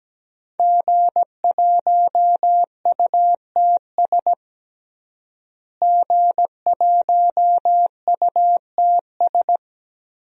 Morse code, heard as a keyed tone, about 17 wpm, 700 Hz. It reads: G1UTS G1UTS
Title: DXCC-100 Call Signs - 5 Characters - Repeated 2x - 17wpm